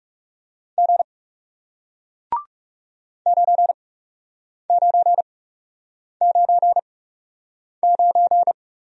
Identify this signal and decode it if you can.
Morse code, heard as a keyed tone, about 40 wpm, spaced as for 8 wpm, 700 Hz. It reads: G 9999